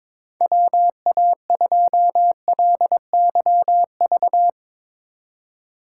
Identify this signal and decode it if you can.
Morse code, heard as a keyed tone, about 22 wpm, 700 Hz. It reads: WA2LYV
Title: USA Call Signs - 6 Characters - 4x Character Spacing - 22wpm